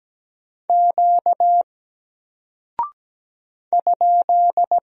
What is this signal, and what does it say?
Morse code, heard as a keyed tone, about 17 wpm, 700 Hz. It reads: Q ?
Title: Single Character (Letter:Number:Punctuation:Prosign) Alt 17wpm